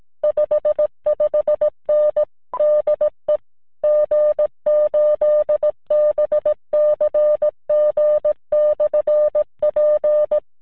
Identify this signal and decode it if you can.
Morse code, heard as a keyed tone, about 17 wpm, 600 Hz. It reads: 55N DE G8BCG/P
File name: VP9-N0JK-pile